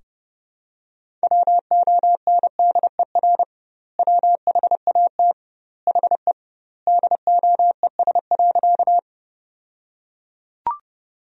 Morse code, heard as a keyed tone, about 30 wpm, 700 Hz. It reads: WONDER WHAT HE DOES.
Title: 2x Word-Spacing - Sentences from Top 500 Words - 30wpm